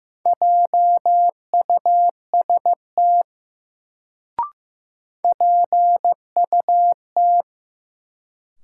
Morse code, heard as a keyed tone, about 15 wpm, 700 Hz. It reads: JUST PUT
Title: Sentences from Top 200 Words - New Word Drill - 15wpm